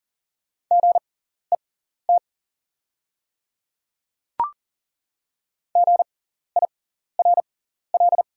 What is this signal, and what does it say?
Morse code, heard as a keyed tone, about 40 wpm, spaced as for 14 wpm, 690 Hz. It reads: GET GIRL
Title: 6x Character Spacing - Top 500 Words - 40wpm